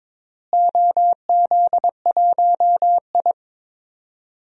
Morse code, heard as a keyed tone, about 22 wpm, 700 Hz. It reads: OZ1I
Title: DXCC-100 Call Signs - 4 Characters - Repeated 2x - 22wpm